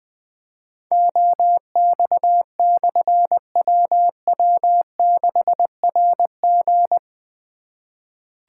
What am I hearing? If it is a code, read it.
Morse code, heard as a keyed tone, about 20 wpm, 700 Hz. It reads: OX/WW6RG
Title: Difficult Call Signs 20wpm